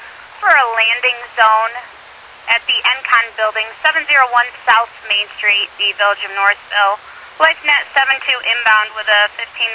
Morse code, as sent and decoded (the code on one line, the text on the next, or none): none